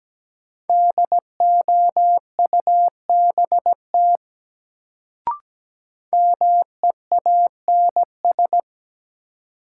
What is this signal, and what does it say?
Morse code, heard as a keyed tone, about 17 wpm, 700 Hz. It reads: DOUBT MEANS
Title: Sets of 5 Words - New Word Drill - 17wpm